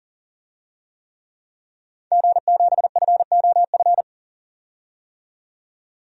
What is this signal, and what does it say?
Morse code, heard as a keyed tone, about 40 wpm, 700 Hz. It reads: G7FOF
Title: DXCC-100 Call Signs - 5 Characters - 40wpm